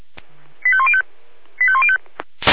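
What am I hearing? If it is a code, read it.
Five-tone selective call (ZVEI), heard as quick runs of five tones, tones of about 70 ms. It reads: 85105 85105
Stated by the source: ZVEI five-tone selective call 85105 received via SDR.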